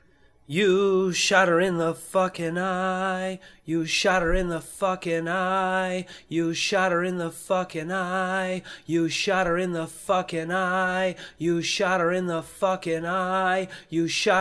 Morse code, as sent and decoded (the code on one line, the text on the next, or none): none